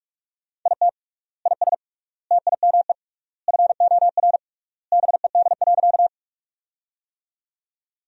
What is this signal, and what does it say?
Morse code, heard as a keyed tone, about 41 wpm, 700 Hz. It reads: IT IS TIME FOR BED.
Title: 3x Word-Spacing - Sentences from Top 500 Words - 45wpm